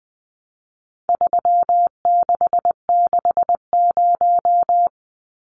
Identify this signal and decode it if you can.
Morse code, heard as a keyed tone, about 20 wpm, 700 Hz. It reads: 3660
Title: Numbers - 4-Digit 20wpm